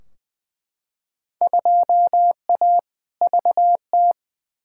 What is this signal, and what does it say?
Morse code, heard as a keyed tone, about 20 wpm, 700 Hz. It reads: 2A VT
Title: Field Day Exchanges Encore - 20wpm